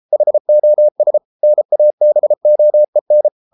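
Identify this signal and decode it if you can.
Morse code, heard as a keyed tone, about 33 wpm, 600 Hz. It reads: HOS NABOEN
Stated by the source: Morse code at 33 wpm, QRQ.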